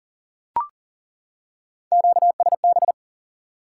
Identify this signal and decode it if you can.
Morse code, heard as a keyed tone, about 40 wpm, 700 Hz. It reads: QSB
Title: Top 90 QSO Elements 40wpm